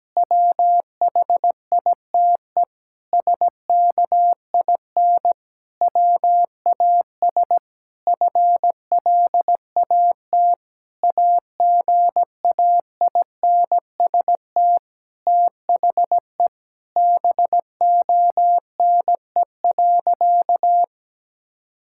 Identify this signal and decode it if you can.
Morse code, heard as a keyed tone, about 17 wpm, 700 Hz. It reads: WHITE SKIN WAS FLAT AGAINST THE BONE.
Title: Sentences from Top 1600 Words 17wpm